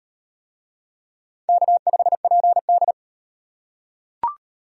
Morse code, heard as a keyed tone, about 36 wpm, 700 Hz. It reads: K5PD